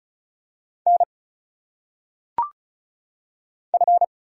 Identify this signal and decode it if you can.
Morse code, heard as a keyed tone, about 36 wpm, 700 Hz. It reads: N F
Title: Single Letters - Phonetic Pronunciation 35wpm